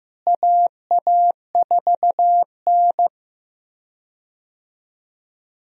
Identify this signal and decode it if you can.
Morse code, heard as a keyed tone, about 15 wpm, 700 Hz. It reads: AA4N
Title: USA Call Signs - 4 Characters - Repeated 3x - 15wpm